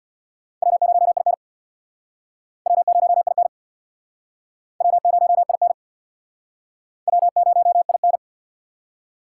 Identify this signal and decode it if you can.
Morse code, heard as a keyed tone, about 66 wpm, 700 Hz. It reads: W0IN W0IN W0IN W0IN